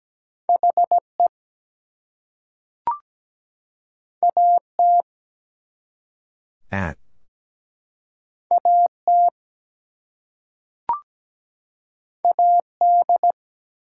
Morse code, heard as a keyed tone, about 17 wpm, 700 Hz. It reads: HE AT AT AD
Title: Two Letter Words - 17wpm